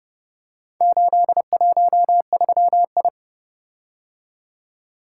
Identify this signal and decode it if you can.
Morse code, heard as a keyed tone, about 29 wpm, 700 Hz. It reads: 813I